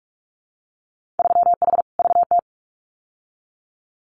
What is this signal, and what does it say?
Morse code, heard as a keyed tone, about 46 wpm, 700 Hz. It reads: 3H VT
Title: Winter Field Day Exchanges Encore 45wpm